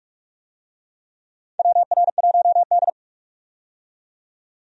Morse code, heard as a keyed tone, about 45 wpm, 700 Hz. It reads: WR1D
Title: USA Call Signs - 4 Characters - 45wpm